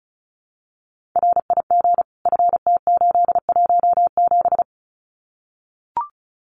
Morse code, heard as a keyed tone, about 36 wpm, 700 Hz. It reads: RIG FT817
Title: Common QSO Phrases Encore 35wpm